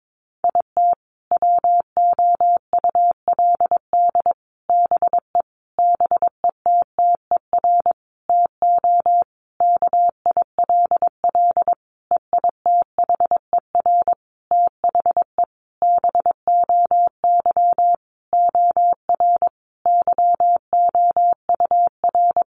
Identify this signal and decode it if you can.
Morse code, heard as a keyed tone, about 22 wpm, 700 Hz. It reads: IT WOULD BE BETTER TO KILL EITHER THE BOY OR YOUR
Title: Sentences from Top 1500 Words 22wpm